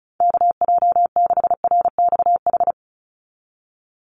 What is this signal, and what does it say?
Morse code, heard as a keyed tone, about 34 wpm, 700 Hz. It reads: KJ6RXH
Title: USA Call Signs 35wpm